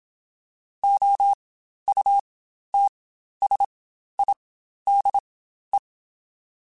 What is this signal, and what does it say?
Morse code, heard as a keyed tone, about 27 wpm, spaced as for 12 wpm, 800 Hz. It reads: OUTSIDE
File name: Spaced Characters / Spaced Characters-mp3s / Weather 5 12 sp @25